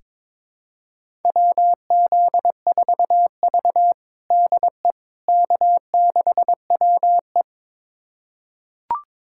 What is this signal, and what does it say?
Morse code, heard as a keyed tone, about 22 wpm, 700 Hz. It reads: WZ4V DE K6WE
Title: USA Callsign DE Callsign - 4 Characters - 22wpm